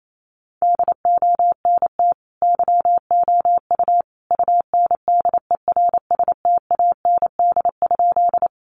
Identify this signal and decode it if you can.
Morse code, heard as a keyed tone, about 28 wpm, 700 Hz. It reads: DONT YOU UNDERSTAND?
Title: Sentences from Top 600 Words 28wpm